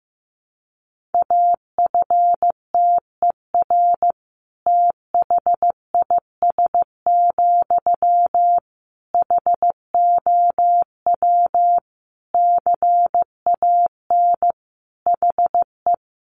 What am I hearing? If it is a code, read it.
Morse code, heard as a keyed tone, about 15 wpm, 700 Hz. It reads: AFTER THIS, HOW CAN HE
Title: Sentences from Top 1500 Words 15wpm